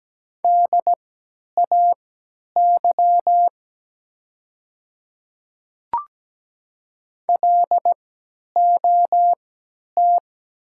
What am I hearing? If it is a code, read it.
Morse code, heard as a keyed tone, about 17 wpm, spaced as for 10 wpm, 700 Hz. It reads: DAY LOT
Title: 3x Character Spacing - Top 500 Words - 17wpm